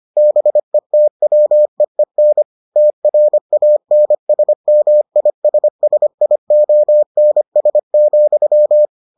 Morse code, heard as a keyed tone, about 25 wpm, 600 Hz. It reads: BETWEEN TRANSMISSIONS,